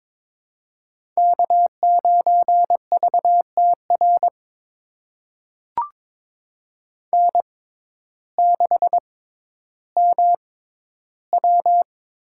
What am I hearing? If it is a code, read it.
Morse code, heard as a keyed tone, about 22 wpm, spaced as for 12 wpm, 700 Hz. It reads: K9VTR N6MW
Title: USA Call Signs - 5 Characters - 6x Character Spacing - 22wpm